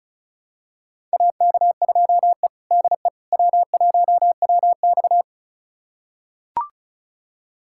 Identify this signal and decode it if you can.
Morse code, heard as a keyed tone, about 34 wpm, 700 Hz. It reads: AK2E DE W1WX
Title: USA Callsign DE Callsign - 4 Characters - 35wpm